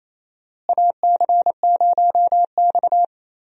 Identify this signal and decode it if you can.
Morse code, heard as a keyed tone, about 27 wpm, 700 Hz. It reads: AC0X
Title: USA Call Signs - 4 Characters - Repeated 3x - 28wpm